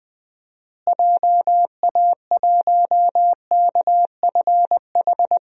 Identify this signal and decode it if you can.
Morse code, heard as a keyed tone, about 21 wpm, 700 Hz. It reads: JA1KFH
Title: DXCC-100 Call Signs - 6 Characters - Repeated 3x - 20wpm